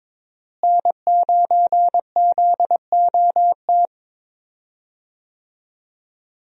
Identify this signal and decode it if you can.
Morse code, heard as a keyed tone, about 22 wpm, 700 Hz. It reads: N9ZOT
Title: USA Call Signs - 5 Characters - Repeated 2x - 22wpm